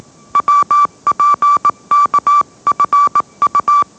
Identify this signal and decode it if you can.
Morse code, heard as a keyed tone, about 19 wpm, 1160 Hz. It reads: WPKFU